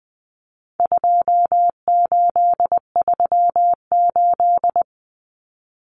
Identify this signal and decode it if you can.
Morse code, heard as a keyed tone, about 20 wpm, 700 Hz. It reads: 2838
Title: Common Cut Numbers - 4-Digit 20wpm